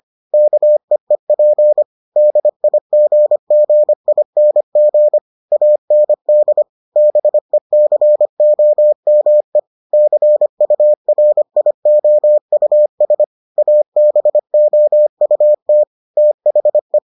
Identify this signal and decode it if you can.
Morse code, heard as a keyed tone, about 25 wpm, 600 Hz. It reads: KEEP DIGGING AND BECOME CURIOUS ABOUT THE